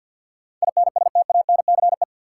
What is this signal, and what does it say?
Morse code, heard as a keyed tone, about 48 wpm, 700 Hz. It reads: INSTANCE
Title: Sentences from Top 800 Words - New Word Drill - 50wpm